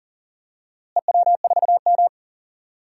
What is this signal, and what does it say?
Morse code, heard as a keyed tone, about 37 wpm, 700 Hz. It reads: EW4M
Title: DXCC-100 Call Signs - 4 Characters - Repeated 3x - 40wpm